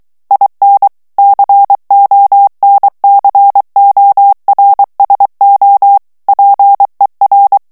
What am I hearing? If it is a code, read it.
Morse code, heard as a keyed tone, about 23 wpm, 800 Hz. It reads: IN CONCORSO PER
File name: ITALIEN texte clair 2 vitesse 22 mots minute